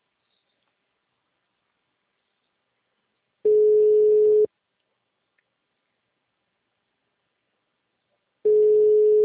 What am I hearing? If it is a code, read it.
Ringback tone (Europe, 425 Hz).